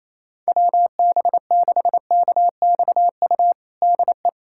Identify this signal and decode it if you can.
Morse code, heard as a keyed tone, about 28 wpm, 700 Hz. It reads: WB6KXU DE
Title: USA Callsign DE Callsign 28wpm